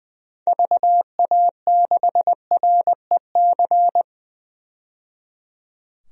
Morse code, heard as a keyed tone, about 20 wpm, 700 Hz. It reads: VA6REC
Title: DXCC-100 Call Signs - 6 Characters - 20wpm